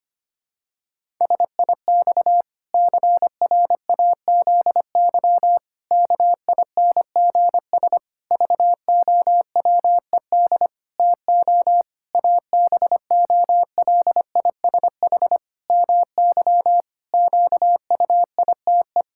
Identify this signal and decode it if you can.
Morse code, heard as a keyed tone, about 25 wpm, 700 Hz. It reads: SIX CRAZY KINGS VOWED TO ABOLISH MY QUITE
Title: Pangram Challenge v2 - 25wpm